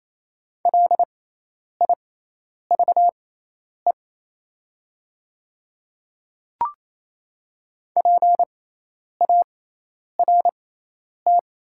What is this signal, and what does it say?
Morse code, heard as a keyed tone, about 28 wpm, spaced as for 10 wpm, 700 Hz. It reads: LIVE PART
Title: 6x Character Spacing - Top 300 Words - 28wpm